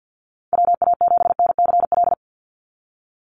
Morse code, heard as a keyed tone, about 48 wpm, 700 Hz. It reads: WA7NCL
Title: NA SOTA Activator Callsigns 50wpm